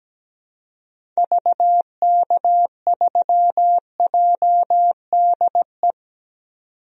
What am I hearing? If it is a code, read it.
Morse code, heard as a keyed tone, about 17 wpm, 700 Hz. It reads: VK3JDE
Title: DXCC-100 Call Signs - 6 Characters - 17wpm